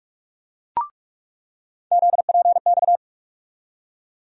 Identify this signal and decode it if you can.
Morse code, heard as a keyed tone, about 46 wpm, 700 Hz. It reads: ZPX
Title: USA Call Sign Postfixes - 3 Letters 45wpm